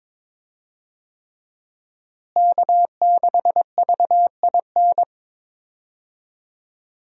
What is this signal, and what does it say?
Morse code, heard as a keyed tone, about 22 wpm, 700 Hz. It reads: K6VIN
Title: USA Call Signs - 5 Characters - Repeated 3x - 22wpm